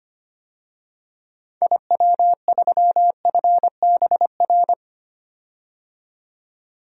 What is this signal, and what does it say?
Morse code, heard as a keyed tone, about 25 wpm, 700 Hz. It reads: IW3FBR
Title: DXCC-100 Call Signs - 6 Characters - 25wpm